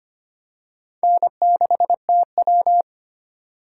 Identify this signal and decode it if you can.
Morse code, heard as a keyed tone, about 25 wpm, 700 Hz. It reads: N6TW
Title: K1USN SST QSO Practice - 25wpm